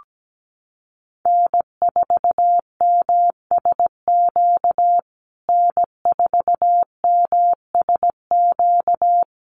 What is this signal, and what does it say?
Morse code, heard as a keyed tone, about 17 wpm, 700 Hz. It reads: N4MSQ N4MSQ
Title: USA Call Signs Repeated 2x 17wpm